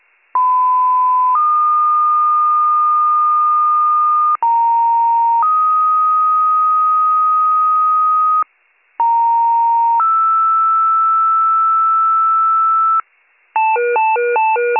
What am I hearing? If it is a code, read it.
Two-tone page, heard as a one-second tone, then a three-second tone, about 1010 then 1220 Hz, about 930 then 1250 Hz, about 930 then 1320 Hz. It is heard from 0.3 to 13.0 s.